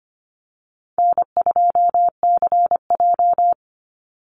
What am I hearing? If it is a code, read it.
Morse code, heard as a keyed tone, about 25 wpm, 700 Hz. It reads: N2CJ
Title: Easy Call Signs Repeated 3x - 25wpm